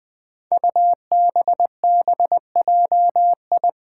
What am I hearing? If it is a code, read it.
Morse code, heard as a keyed tone, about 20 wpm, 700 Hz. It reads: UBBJI
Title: Five Challenging Letter-Number Combinations 20wpm